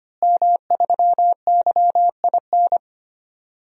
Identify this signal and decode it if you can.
Morse code, heard as a keyed tone, about 25 wpm, 700 Hz. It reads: M3YIN
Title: DXCC-100 Call Signs - 5 Characters - Repeated 2x - 25wpm